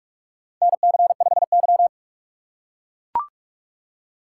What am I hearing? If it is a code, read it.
Morse code, heard as a keyed tone, about 44 wpm, 700 Hz. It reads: NC5Y
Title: USA Call Signs - 4 Characters - 6x Character Spacing - 45wpm